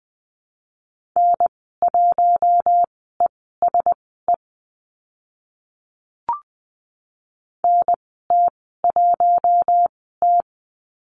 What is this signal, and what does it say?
Morse code, heard as a keyed tone, about 20 wpm, spaced as for 14 wpm, 700 Hz. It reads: N1ESE NT1T
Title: 042 Lesson 8g - Number 1 - Call Signs 20wpm